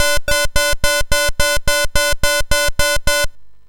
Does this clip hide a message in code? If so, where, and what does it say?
No.